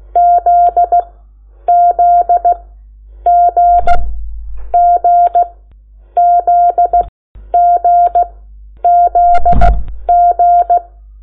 Morse code, heard as a keyed tone, about 16 wpm, spaced as for 9 wpm, 700 Hz. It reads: ZZG GZGZG
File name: Lecciones / CLASE Nº8